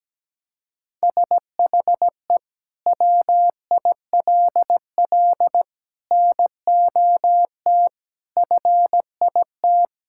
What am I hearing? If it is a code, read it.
Morse code, heard as a keyed tone, about 17 wpm, 700 Hz. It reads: SHE WILL NOT FIT
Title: Sentences from Top 1000 Words - Review Part 5 - 17wpm